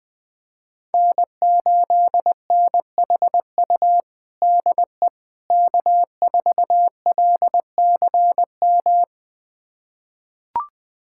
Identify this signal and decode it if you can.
Morse code, heard as a keyed tone, about 20 wpm, 700 Hz. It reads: N8NHU DE K4LCM
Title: USA Callsign DE Callsign - 5 Characters - 20wpm